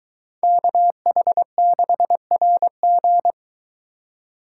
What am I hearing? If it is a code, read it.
Morse code, heard as a keyed tone, about 23 wpm, 700 Hz. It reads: KH6RG